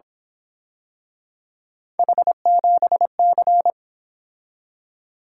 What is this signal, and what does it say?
Morse code, heard as a keyed tone, about 26 wpm, 700 Hz. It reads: H7C
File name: Speed-Racing - Three Letter-Number Combinations - 22wpm